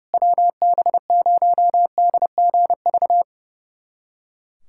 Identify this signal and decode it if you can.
Morse code, heard as a keyed tone, about 30 wpm, 700 Hz. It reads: WB0DGV
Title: USA Callsign DE Callsign 30wpm